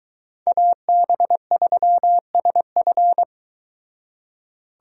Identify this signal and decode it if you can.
Morse code, heard as a keyed tone, about 22 wpm, 700 Hz. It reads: AB3SF